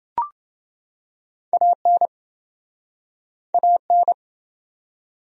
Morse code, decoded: AN AN

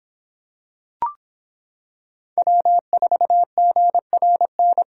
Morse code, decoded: W4GRN

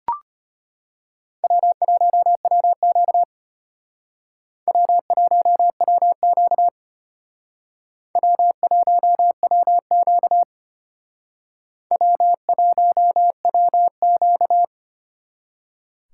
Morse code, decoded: W1WQ W1WQ W1WQ W1WQ

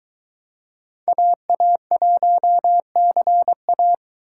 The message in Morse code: AA1CA

